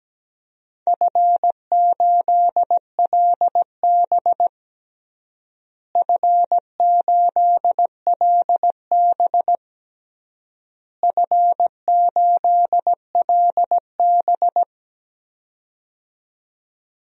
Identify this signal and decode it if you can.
Morse code, heard as a keyed tone, about 17 wpm, 700 Hz. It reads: F8LB F8LB F8LB
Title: DXCC-100 Call Signs - 4 Characters - Repeated 3x - 17wpm